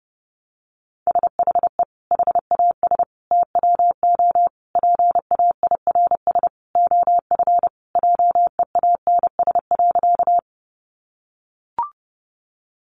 Morse code, heard as a keyed tone, about 30 wpm, 700 Hz. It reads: SHE HAS TWO PAIRS OF JEANS.
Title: Additional Easy Phrases 30wpm